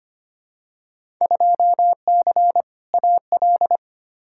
Morse code, heard as a keyed tone, about 25 wpm, 700 Hz. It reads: 2C AL
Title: Field Day Exchanges Encore - 25wpm